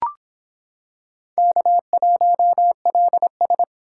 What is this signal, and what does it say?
Morse code, heard as a keyed tone, about 23 wpm, 700 Hz. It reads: K1LS